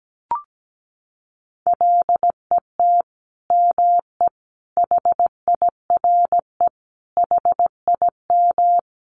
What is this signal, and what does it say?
Morse code, heard as a keyed tone, about 17 wpm, 700 Hz. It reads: LET ME HIRE HIM
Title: Sentences from Top 1400 Words 17wpm